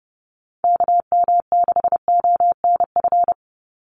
Morse code, heard as a keyed tone, about 30 wpm, 700 Hz. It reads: KM6ONF
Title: USA Call Signs 30wpm